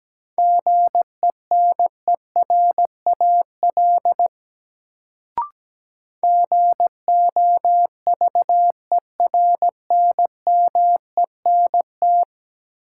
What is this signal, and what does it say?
Morse code, heard as a keyed tone, about 17 wpm, 700 Hz. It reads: GENERAL GOVERNMENT
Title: Sentences from Top 600 Words - New Word Drill - 17wpm